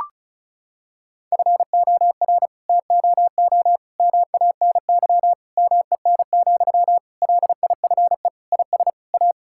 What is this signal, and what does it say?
Morse code, heard as a keyed tone, about 36 wpm, 700 Hz. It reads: FOR TOO MANY MEN, LIFE IS A
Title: Sentences from Top 1000 Words - Review Part 6 - 35wpm